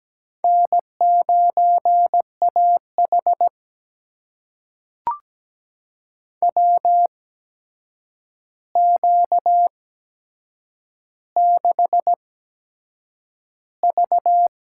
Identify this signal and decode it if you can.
Morse code, heard as a keyed tone, about 17 wpm, spaced as for 7 wpm, 700 Hz. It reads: N9AH WQ6V
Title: USA Call Signs - 4 Characters - 8x Character Spacing - 17wpm